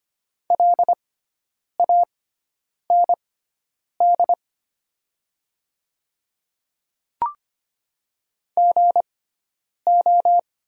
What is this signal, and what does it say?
Morse code, heard as a keyed tone, about 25 wpm, spaced as for 9 wpm, 700 Hz. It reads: LAND GO
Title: 6x Character Spacing - Top 300 Words - 25wpm